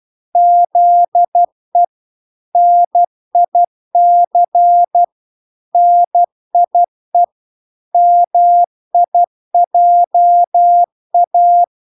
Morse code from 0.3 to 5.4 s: ZE NIC